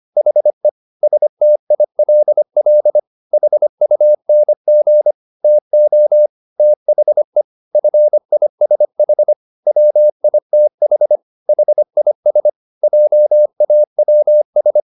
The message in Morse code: HE STILL HUNG TO THE FISH WITH HIS JAWS

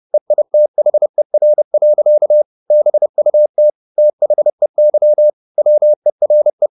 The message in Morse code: EITHER. BUT THEY WERE

